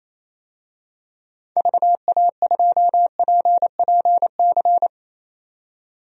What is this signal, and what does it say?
Morse code, heard as a keyed tone, about 28 wpm, 700 Hz. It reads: VA2PPC
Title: DXCC-100 Call Signs - 6 Characters - 28wpm